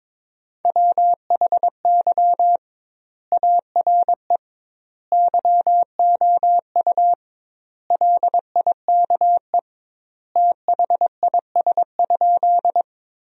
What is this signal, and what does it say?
Morse code, heard as a keyed tone, about 22 wpm, 700 Hz. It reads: WHY ARE YOU LIKE THIS?
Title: 2x Word-Spacing - Sentences from Top 200 Words - 22wpm